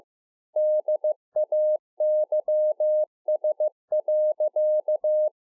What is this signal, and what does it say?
Morse code, heard as a keyed tone, about 15 wpm, 610 Hz. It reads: DAYS.
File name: 15WPM